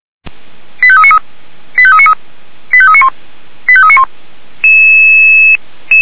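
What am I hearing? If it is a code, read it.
Five-tone selective call (ZVEI), heard as quick runs of five tones, tones of about 70 ms. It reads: 85393 85393 85391 85391
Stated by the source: Five-tone ZVEI selective call 89145 received by SDR, on the feuerwehr-orlamuende.org server.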